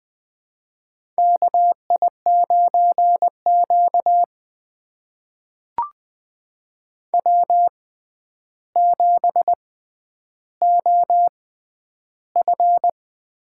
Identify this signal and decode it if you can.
Morse code, heard as a keyed tone, about 20 wpm, spaced as for 10 wpm, 700 Hz. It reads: KI9Q W7OF
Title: USA Call Signs - 4 Characters - 6x Character Spacing - 20wpm